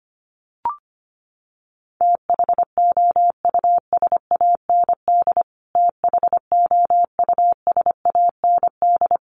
Morse code, heard as a keyed tone, about 25 wpm, 700 Hz. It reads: THOUSAND THOUSAND
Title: Top 500 Words Repeated 25wpm